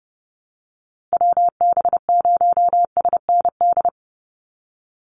Morse code, heard as a keyed tone, about 30 wpm, 700 Hz. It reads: WB0SND
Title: Call Signs Repeated 3x - 30wpm